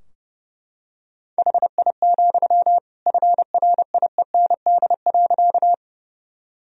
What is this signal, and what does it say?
Morse code, heard as a keyed tone, about 30 wpm, 700 Hz. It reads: HI, FRIEND.
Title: Sentences from Top 1000 Words - Review Part 3 - 30wpm